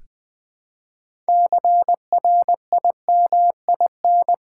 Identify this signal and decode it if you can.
Morse code, heard as a keyed tone, about 20 wpm, 700 Hz. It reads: CRIMIN
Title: Build A Word - 2 Letters At A Time - 8-Letter Words - 20wpm